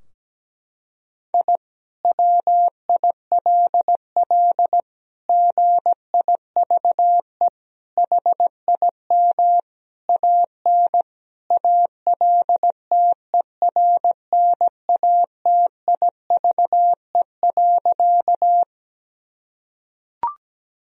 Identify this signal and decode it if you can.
Morse code, heard as a keyed tone, about 17 wpm, 700 Hz. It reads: I WILL GIVE HIM AN ALTERNATIVE.
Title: Sentences from Top 1000 Words - Review Part 4 - 17wpm